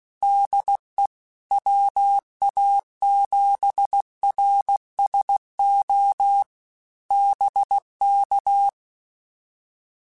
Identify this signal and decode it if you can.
Morse code, heard as a keyed tone, about 16 wpm, 800 Hz. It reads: DE WA7RSO BK
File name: QSOs / QSOs-mp3s / WD0FBT 1@15